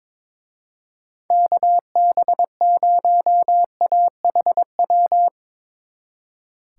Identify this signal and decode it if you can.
Morse code, heard as a keyed tone, about 23 wpm, 700 Hz. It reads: KB0AHW
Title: Speed Racing - USA Call Signs 22wpm